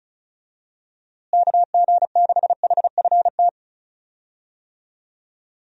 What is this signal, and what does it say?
Morse code, heard as a keyed tone, about 36 wpm, 700 Hz. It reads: KG6HFT
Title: USA Call Signs - 6 Characters - Repeated 3x - 35wpm